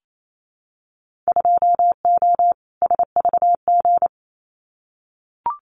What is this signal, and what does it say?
Morse code, heard as a keyed tone, about 28 wpm, 700 Hz. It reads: 2O SVG
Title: Winter Field Day Exchanges 28wpm